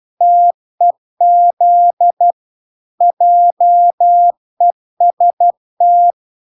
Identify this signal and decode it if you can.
Morse code, heard as a keyed tone, about 12 wpm, 700 Hz. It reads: TEZ JEST